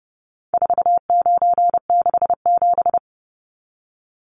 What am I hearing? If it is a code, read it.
Morse code, heard as a keyed tone, about 30 wpm, 700 Hz. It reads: 4967
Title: Numbers - 4-Digit 30wpm